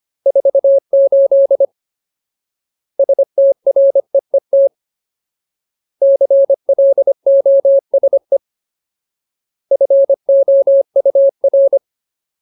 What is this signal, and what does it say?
Morse code, heard as a keyed tone, about 25 wpm, 550 Hz. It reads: 48 STREET CLOSE FOUR
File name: Ultimate Word List / 25WPM Ultimate